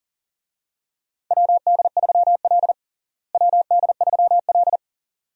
Morse code, heard as a keyed tone, about 40 wpm, 700 Hz. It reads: WD3L WD3L
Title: USA Call Signs - 4 Characters - Repeated 2x - 40wpm